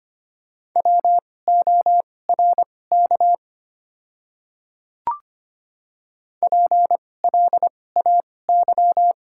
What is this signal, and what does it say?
Morse code, heard as a keyed tone, about 25 wpm, spaced as for 18 wpm, 700 Hz. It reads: WORK PLAY
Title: 2x Character Spacing - Top 200 Words - 25wpm